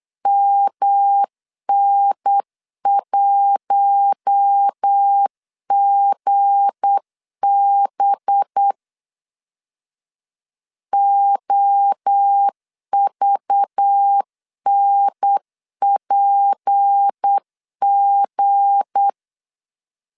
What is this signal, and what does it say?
Morse code, heard as a keyed tone, about 8 wpm, 800 Hz. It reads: MN1GB OVNPG
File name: LEZIONI / 7 GF9Z / 7 G 8wpm 3118